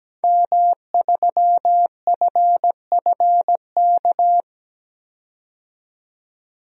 Morse code, heard as a keyed tone, about 17 wpm, 700 Hz. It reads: M3FFK
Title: DXCC-100 Call Signs - 5 Characters - 17wpm